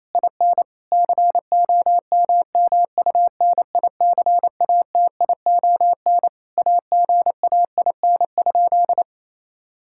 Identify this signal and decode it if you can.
Morse code, heard as a keyed tone, about 28 wpm, 700 Hz. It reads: IN COMMUNICATION AGAIN?
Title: Sentences from Top 1000 Words - Review Part 6 - 28wpm